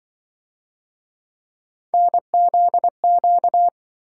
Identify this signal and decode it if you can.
Morse code, heard as a keyed tone, about 26 wpm, 700 Hz. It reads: NZQ